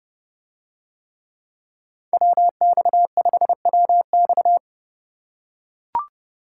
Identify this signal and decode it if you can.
Morse code, heard as a keyed tone, about 30 wpm, 700 Hz. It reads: WX5WX